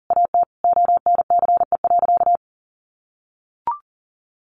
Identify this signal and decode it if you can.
Morse code, heard as a keyed tone, about 40 wpm, 700 Hz. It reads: AT ONCE.
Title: Sentences from Top 1100 Words 40wpm